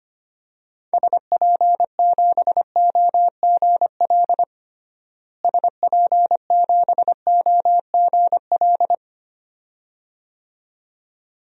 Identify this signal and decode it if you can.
Morse code, heard as a keyed tone, about 25 wpm, 700 Hz. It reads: SP7OGL SP7OGL
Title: DXCC-100 Call Signs - 6 Characters - Repeated 2x - 25wpm